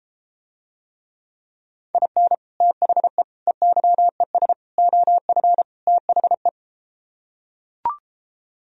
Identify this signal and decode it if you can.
Morse code, heard as a keyed tone, about 33 wpm, 700 Hz. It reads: IN THE EYES OF THE